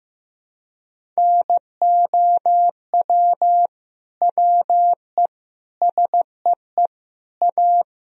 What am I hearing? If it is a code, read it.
Morse code, heard as a keyed tone, about 15 wpm, 700 Hz. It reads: NOW WE SEE A
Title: Sentences from Top 1000 Words - Review Part 1 - 15wpm